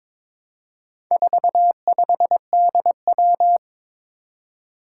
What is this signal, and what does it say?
Morse code, heard as a keyed tone, about 22 wpm, 700 Hz. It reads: 45DW
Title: Four Challenging Letter-Number Combinations 22wpm